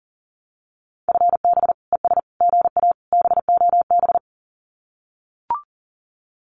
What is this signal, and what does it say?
Morse code, heard as a keyed tone, about 41 wpm, 700 Hz. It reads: FB ES GA BOB
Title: Common QSO Phrases Encore 40wpm